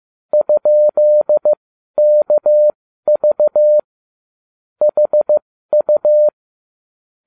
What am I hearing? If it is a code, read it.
Morse code, heard as a keyed tone, about 15 wpm, spaced as for 12 wpm, 600 Hz. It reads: ?KV HU